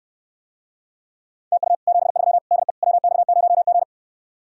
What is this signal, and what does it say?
Morse code, heard as a keyed tone, about 69 wpm, 700 Hz. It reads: TU 73 DE WX0G